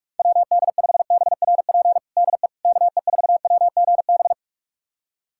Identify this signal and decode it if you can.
Morse code, heard as a keyed tone, about 46 wpm, 700 Hz. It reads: WD5BRP DE KE4WGB
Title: USA Callsign DE Callsign 45wpm